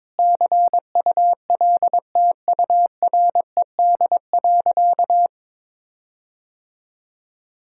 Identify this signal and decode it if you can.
Morse code, heard as a keyed tone, about 22 wpm, 700 Hz. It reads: CULTURED.
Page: morse-code-ninja-practice-22wpm